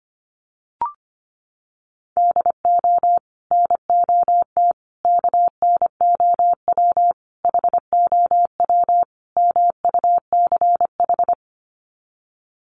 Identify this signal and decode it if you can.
Morse code, heard as a keyed tone, about 25 wpm, 700 Hz. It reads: DO NOT KNOW HOW MUCH
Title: Sets of 5 Words Encore 25wpm